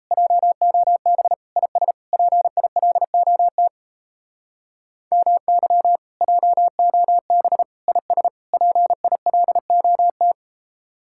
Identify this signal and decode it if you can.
Morse code, heard as a keyed tone, about 34 wpm, 700 Hz. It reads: JOB IS PILOT MY JOB IS PILOT
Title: Speed-Racing Common QSO Phrases Encore - 28wpm